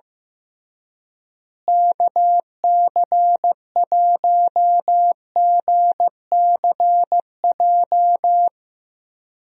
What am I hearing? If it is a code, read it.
Morse code, heard as a keyed tone, about 15 wpm, 700 Hz. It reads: KC1GCJ
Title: USA Call Signs - 6 Characters - Repeated 3x - 15wpm